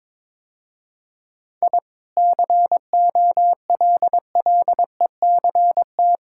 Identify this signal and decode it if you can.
Morse code, heard as a keyed tone, about 22 wpm, 700 Hz. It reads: I COLLECT
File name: practice / 22wpm / Easy Sentences with Hints - 22wpm